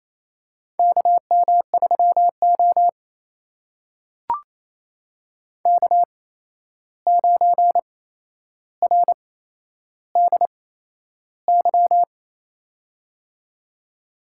Morse code, heard as a keyed tone, about 28 wpm, spaced as for 11 wpm, 700 Hz. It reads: KM3O K9RDY
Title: USA Call Signs - 8x Character Spacing - 28wpm